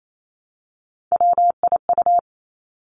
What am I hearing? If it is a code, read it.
Morse code, heard as a keyed tone, about 28 wpm, 700 Hz. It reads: WIU
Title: Three Challenging Letter-Number Combinations 28wpm